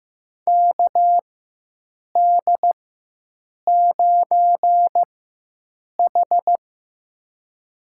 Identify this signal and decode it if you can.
Morse code, heard as a keyed tone, about 15 wpm, spaced as for 7 wpm, 700 Hz. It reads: KD9H
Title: USA Call Signs - 4 Characters - 4x Character Spacing - 15wpm